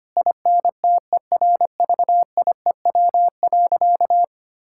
Morse code, heard as a keyed tone, about 25 wpm, 700 Hz. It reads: INTERVIEW.
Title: Sentences from Top 1600 Words 25wpm